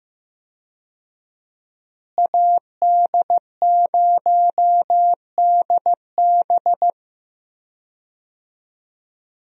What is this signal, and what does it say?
Morse code, heard as a keyed tone, about 15 wpm, 700 Hz. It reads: AD0DB
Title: USA Call Signs - 5 Characters - Repeated 2x - 15wpm